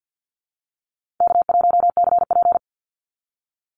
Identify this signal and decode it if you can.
Morse code, heard as a keyed tone, about 48 wpm, 700 Hz. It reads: K1CP
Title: Easy Call Signs - 50wpm